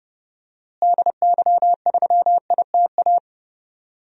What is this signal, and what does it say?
Morse code, heard as a keyed tone, about 31 wpm, 700 Hz. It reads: DY3ITA
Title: DXCC-100 Call Signs - 6 Characters - Repeated 3x - 30wpm